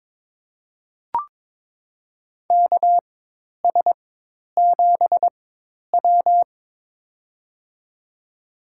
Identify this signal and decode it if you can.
Morse code, heard as a keyed tone, about 22 wpm, spaced as for 10 wpm, 700 Hz. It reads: KS7W
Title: USA Call Signs - 4 Characters - 4x Character Spacing - 22wpm